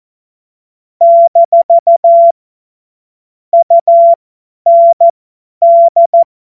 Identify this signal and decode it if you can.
Morse code, heard as a keyed tone, about 14 wpm, spaced as for 10 wpm, 680 Hz. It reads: - UND
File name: Morsekurs / Witze / Witz 01. November 2022 50 F70